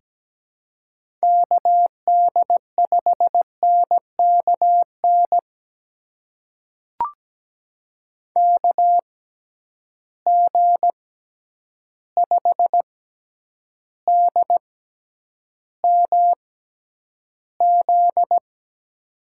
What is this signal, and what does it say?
Morse code, heard as a keyed tone, about 17 wpm, spaced as for 9 wpm, 700 Hz. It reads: KD5NKN KG5DMZ
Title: USA Call Signs - 6 Characters - 6x Character Spacing - 17wpm